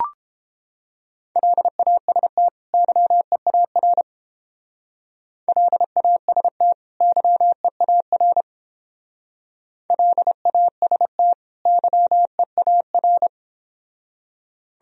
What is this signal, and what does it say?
Morse code, heard as a keyed tone, about 29 wpm, 700 Hz. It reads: LAST YEAR LAST YEAR LAST YEAR